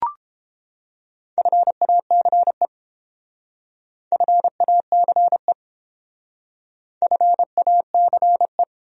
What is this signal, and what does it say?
Morse code, heard as a keyed tone, about 29 wpm, 700 Hz. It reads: FACE FACE FACE